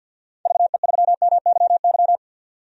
Morse code, heard as a keyed tone, about 51 wpm, 700 Hz. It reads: VE3MYY
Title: DXCC-100 Call Signs - 6 Characters - 50wpm